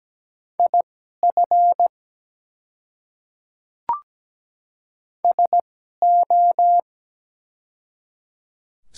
Morse code, heard as a keyed tone, about 17 wpm, spaced as for 12 wpm, 700 Hz. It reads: IF SO
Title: 2x Character Spacing - Top 200 Words - 17wpm